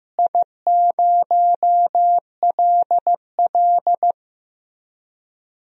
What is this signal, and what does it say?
Morse code, heard as a keyed tone, about 15 wpm, 700 Hz. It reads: I0LL
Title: DXCC-100 Call Signs - 4 Characters - Repeated 2x - 15wpm